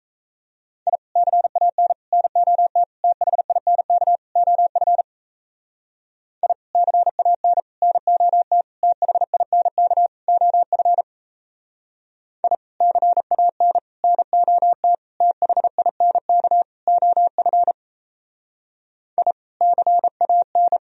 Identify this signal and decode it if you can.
Morse code, heard as a keyed tone, about 36 wpm, 700 Hz. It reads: I CAN NOT THINK OF I CAN NOT THINK OF I CAN NOT THINK OF I CAN